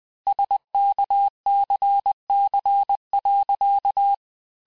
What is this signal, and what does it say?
Morse code, heard as a keyed tone, about 20 wpm, 800 Hz. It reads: SKCC.
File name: cours-du-mercredi / audios / cours-870-FIN-de-QSO-vitesse-19-MPM